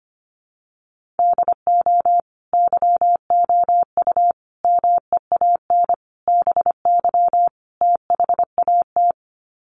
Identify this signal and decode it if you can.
Morse code, heard as a keyed tone, about 25 wpm, 700 Hz. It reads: DO YOU MEAN BY THAT
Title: Sets of 5 Words Encore2 25wpm